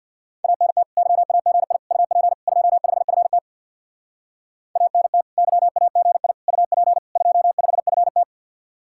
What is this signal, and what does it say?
Morse code, heard as a keyed tone, about 62 wpm, 700 Hz. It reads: ANT YAGI UP 25FT ANT YAGI UP 25FT
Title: Speed-Racing Common QSO Phrases - 50wpm